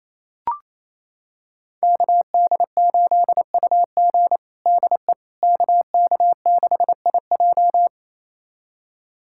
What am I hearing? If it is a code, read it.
Morse code, heard as a keyed tone, about 28 wpm, 700 Hz. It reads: KD8UG DE KK6IJ
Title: USA Callsign DE Callsign - 5 Characters - 28wpm